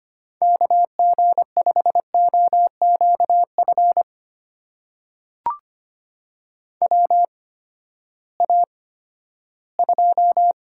Morse code, heard as a keyed tone, about 25 wpm, spaced as for 14 wpm, 700 Hz. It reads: KG5OQF WA2
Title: USA Call Signs - 6 Characters - 8x Character Spacing - 25wpm